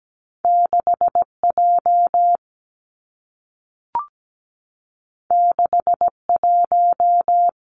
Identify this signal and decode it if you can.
Morse code, heard as a keyed tone, about 17 wpm, 700 Hz. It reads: 6 J 6 1
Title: Two Challenging Letter-Number Combinations 17wpm